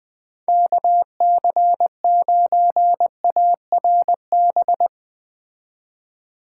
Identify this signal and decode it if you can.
Morse code, heard as a keyed tone, about 20 wpm, 700 Hz. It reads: KC9ARB
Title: Speed Racing - USA Call Signs 17wpm